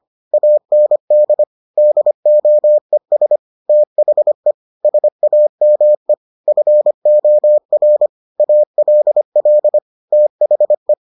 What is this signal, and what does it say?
Morse code, heard as a keyed tone, about 25 wpm, 600 Hz. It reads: AND DOES THE SAME FOR ALL THE